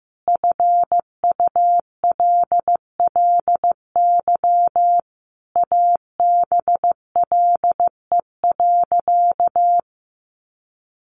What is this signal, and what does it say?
Morse code, heard as a keyed tone, about 15 wpm, 700 Hz. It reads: FULLY ABLE.
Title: Sentences from Top 1300 Words 15wpm